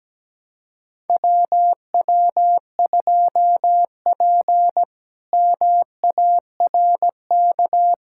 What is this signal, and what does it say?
Morse code, heard as a keyed tone, about 17 wpm, 700 Hz. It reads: WW2P MARK
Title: K1USN SST QSO Practice - 17wpm